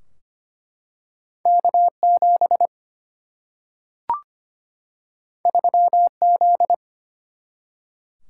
Morse code, heard as a keyed tone, about 25 wpm, 700 Hz. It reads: K 7 3 Z
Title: Single-Letter Single-Number 25wpm